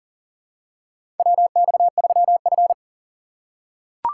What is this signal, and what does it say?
Morse code, heard as a keyed tone, about 38 wpm, 690 Hz. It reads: WX3F
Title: USA Call Signs - 4 Characters - 2x Character Spacing - 40wpm